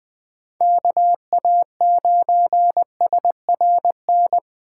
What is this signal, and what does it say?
Morse code, heard as a keyed tone, about 20 wpm, 700 Hz. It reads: KA9SRN
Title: USA Call Signs - 6 Characters - 8x Character Spacing - 20wpm